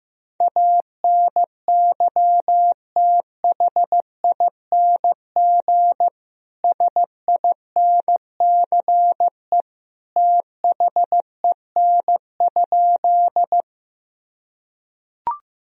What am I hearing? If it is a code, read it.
Morse code, heard as a keyed tone, about 16 wpm, 700 Hz. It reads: ANYTHING SINCE THEN?
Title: Sentences from Top 1000 Words - Review Part 1 - 15wpm